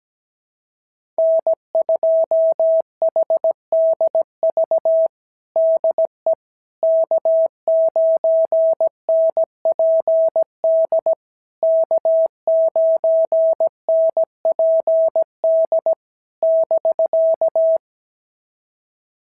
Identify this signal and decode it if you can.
Morse code, heard as a keyed tone, about 16 wpm, 650 Hz. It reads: N2HDV DE K9NPD K9NPD <BK>
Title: SOTA QSO Practice 17wpm